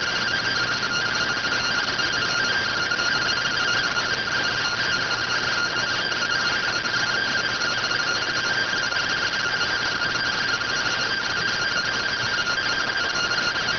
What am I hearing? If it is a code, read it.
Two-tone teleprinter signal, about 170 Hz shift.